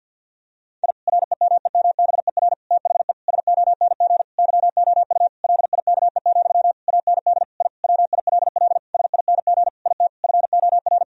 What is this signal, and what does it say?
Morse code, heard as a keyed tone, about 50 wpm, 700 Hz. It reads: I REMEMBER THE SONG YOU LIKE, AND I WILL SING IT FOR
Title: Sentences from Top 1000 Words - Review Part 2 - 50wpm